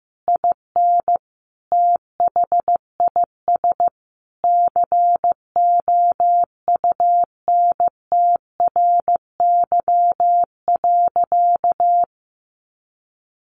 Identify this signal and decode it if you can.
Morse code, heard as a keyed tone, about 15 wpm, 700 Hz. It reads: IN THIS COUNTRY.
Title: Sentences from Top 1200 Words 15wpm